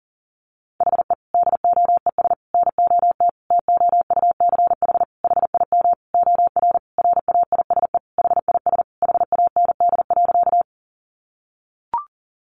Morse code, heard as a keyed tone, about 40 wpm, 700 Hz. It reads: HE DOES NOT TOUCH HIM OR RAISE HIS HAND.
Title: Sentences from Top 1400 Words 40wpm